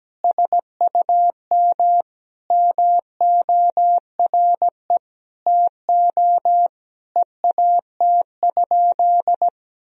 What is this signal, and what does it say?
Morse code, heard as a keyed tone, about 17 wpm, 700 Hz. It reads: SUM MORE TO EAT?
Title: Sentences from Top 300 Words and Abbreviations 17wpm